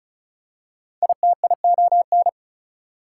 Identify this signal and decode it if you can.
Morse code, heard as a keyed tone, about 36 wpm, 700 Hz. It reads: ITION
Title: Prefixes and Suffixes 35wpm